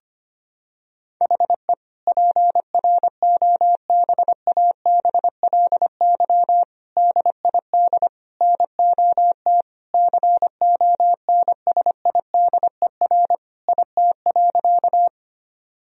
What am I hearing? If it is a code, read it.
Morse code, heard as a keyed tone, about 25 wpm, 700 Hz. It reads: HE PROBABLY DID NOT CONSIDER IT.
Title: Sentences from Top 1000 Words - Review Part 4 - 25wpm